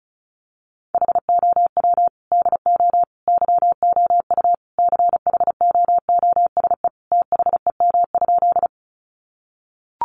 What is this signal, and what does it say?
Morse code, heard as a keyed tone, about 34 wpm, 700 Hz. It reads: HOW DO YOU CHOOSE THEM?
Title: Sentences from Top 700 Words 35wpm